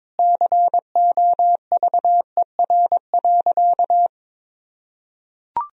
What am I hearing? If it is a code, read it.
Morse code, heard as a keyed tone, about 22 wpm, 700 Hz. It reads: COVER.
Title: Sentences from Top 1000 Words - Review Part 1 - 22wpm